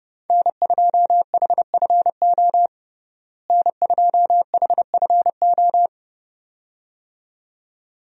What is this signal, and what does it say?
Morse code, heard as a keyed tone, about 30 wpm, 700 Hz. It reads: N2HFO N2HFO
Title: USA Call Signs - 5 Characters - Repeated 2x - 30wpm